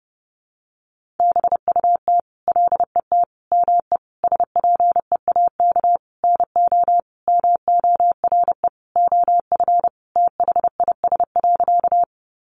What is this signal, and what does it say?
Morse code, heard as a keyed tone, about 30 wpm, 700 Hz. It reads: BUT LET ME SPEAK NO MORE OF THIS.
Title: Sentences from Top 1100 Words 30wpm